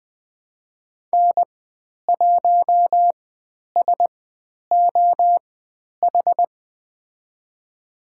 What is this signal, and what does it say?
Morse code, heard as a keyed tone, about 20 wpm, spaced as for 10 wpm, 700 Hz. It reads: N1SOH
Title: 095 Lesson 16g - Letter c - Call Signs 20wpm